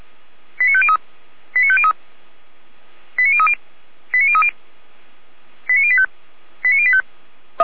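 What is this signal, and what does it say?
Five-tone selective call (ZVEI), heard as quick runs of five tones, tones of about 70 ms. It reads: 89503 89503 89030 89030 89085 89085
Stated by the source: ZVEI five-tone selective call 89080, received on an SDR.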